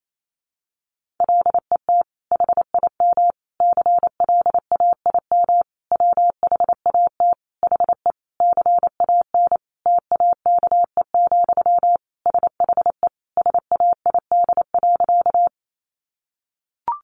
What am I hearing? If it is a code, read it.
Morse code, heard as a keyed tone, about 28 wpm, 700 Hz. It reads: LET HIM CLAIM WHAT HE CAN TAKE, SHE SAID.
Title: Sentences from Top 1500 Words 28wpm